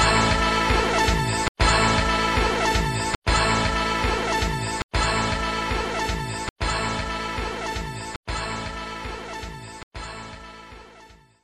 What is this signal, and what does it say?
There is no code.